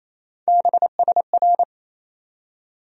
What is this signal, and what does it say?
Morse code, heard as a keyed tone, about 28 wpm, 700 Hz. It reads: BSR
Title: USA Call Sign Postfixes - 3 Letters 28wpm